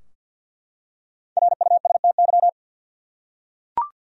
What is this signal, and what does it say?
Morse code, heard as a keyed tone, about 52 wpm, 700 Hz. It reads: RUSTY
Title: USA Common Names - Max 5 Letters - 50wpm